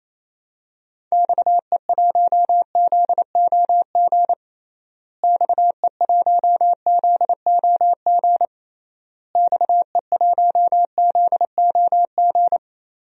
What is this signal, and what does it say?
Morse code, heard as a keyed tone, about 27 wpm, 700 Hz. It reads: XE1ZOG XE1ZOG XE1ZOG
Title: DXCC-100 Call Signs - 6 Characters - Repeated 3x - 28wpm